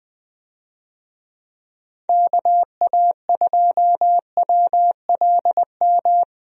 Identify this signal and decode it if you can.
Morse code, heard as a keyed tone, about 20 wpm, 700 Hz. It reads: KA2WLM
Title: USA Call Signs - 6 Characters - Repeated 3x - 20wpm